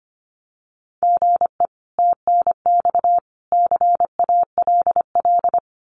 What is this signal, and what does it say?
Morse code, heard as a keyed tone, about 25 wpm, 700 Hz. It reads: GE TNX CALL
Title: Common QSO Phrases 25wpm